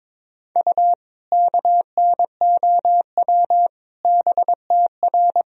Morse code, heard as a keyed tone, about 22 wpm, 700 Hz. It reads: U KNOW BTR
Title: Sentences from Top 400 Words and Abbreviations 22wpm